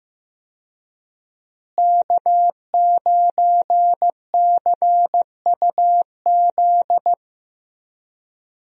Morse code, heard as a keyed tone, about 15 wpm, 700 Hz. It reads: K9CUZ
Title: USA Call Signs - 5 Characters - Repeated 2x - 15wpm